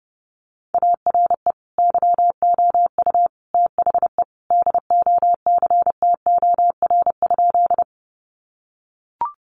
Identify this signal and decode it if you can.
Morse code, heard as a keyed tone, about 28 wpm, 700 Hz. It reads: ARE YOU THE DOCTOR?
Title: Sentences from Top 1500 Words 30wpm